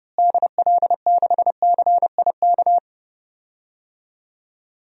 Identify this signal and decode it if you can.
Morse code, heard as a keyed tone, about 30 wpm, 700 Hz. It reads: DL6CIK
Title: DXCC-100 Call Signs - 6 Characters - Repeated 3x - 30wpm